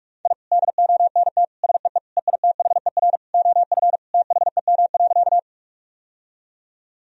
Morse code, heard as a keyed tone, about 46 wpm, 700 Hz. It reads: I DONT SEE EITHER OF THEM.
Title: Sentences from Top 1000 Words - Review Part 4 - 45wpm